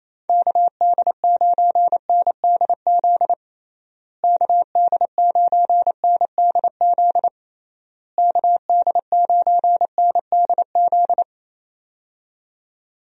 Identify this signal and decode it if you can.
Morse code, heard as a keyed tone, about 27 wpm, 700 Hz. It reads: KD9NDZ KD9NDZ KD9NDZ
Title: USA Call Signs - 6 Characters - Repeated 3x - 28wpm